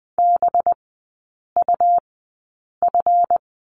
Morse code, heard as a keyed tone, about 20 wpm, spaced as for 8 wpm, 700 Hz. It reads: BUF
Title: 184 Lesson 30e - Letter b - Three Characters 20wpm